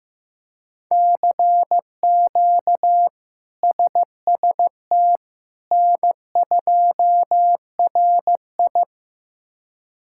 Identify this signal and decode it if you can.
Morse code, heard as a keyed tone, about 15 wpm, 700 Hz. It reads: CQ SST N2RI
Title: K1USN SST QSO Practice - 15wpm